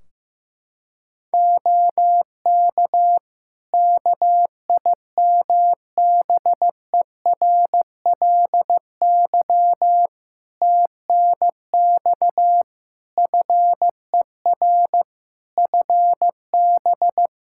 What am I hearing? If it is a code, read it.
Morse code, heard as a keyed tone, about 15 wpm, 700 Hz. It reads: OK KIMBERLY TNX FER FB
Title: Practice QSO - Exchange 4 - Ending - 15wpm